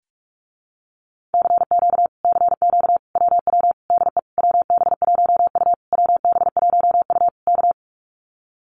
Morse code, heard as a keyed tone, about 46 wpm, 700 Hz. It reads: CQ CQ WW DE WB1U WB1U K
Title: Common QSO Phrases Encore 45wpm